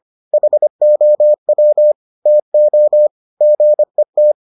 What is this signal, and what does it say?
Morse code, heard as a keyed tone, about 25 wpm, 600 Hz. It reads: HOW TO GET